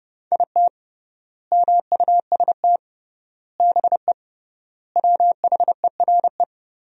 Morse code, heard as a keyed tone, about 30 wpm, 700 Hz. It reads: IT MUST BE WHERE
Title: 3x Word-Spacing - Sentences from Top 200 Words - 30wpm